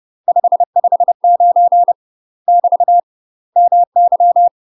Morse code, heard as a keyed tone, about 30 wpm, 700 Hz. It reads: 559 <BT> MY
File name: UK MASTER QSOs TRAINING / 30WPM / UK MASTER QSOs 0047